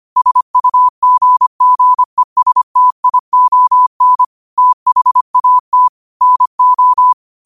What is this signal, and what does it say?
Morse code, heard as a keyed tone, about 25 wpm, 1000 Hz. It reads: SUGGESTION THAT NO
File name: qotdfiles / 25wpm